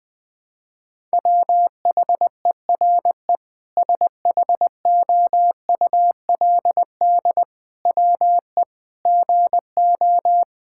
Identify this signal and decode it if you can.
Morse code, heard as a keyed tone, about 20 wpm, 700 Hz. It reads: WHERE SHOULD WE GO
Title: Sentences from Top 1000 Words - Review Part 5 - 20wpm